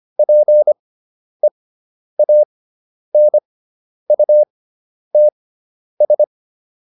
Morse code, heard as a keyed tone, about 25 wpm, spaced as for 10 wpm, 600 Hz. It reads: PEANUTS